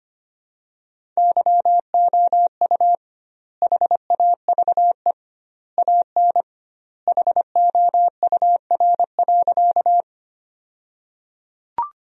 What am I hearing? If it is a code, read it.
Morse code, heard as a keyed tone, about 25 wpm, 700 Hz. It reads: YOU HAVE AN HOUR.
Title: 2x Word-Spacing - Sentences from Top 400 Words - 25wpm